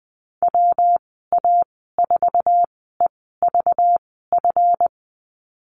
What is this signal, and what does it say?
Morse code, heard as a keyed tone, about 20 wpm, spaced as for 15 wpm, 700 Hz. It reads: WA4EVF
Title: 231 Lesson 37g - Letter x - Call Signs 20wpm